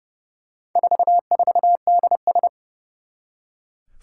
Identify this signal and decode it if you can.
Morse code, heard as a keyed tone, about 28 wpm, 690 Hz. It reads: 44DS
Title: Four Challenging Letter-Number Combinations 30wpm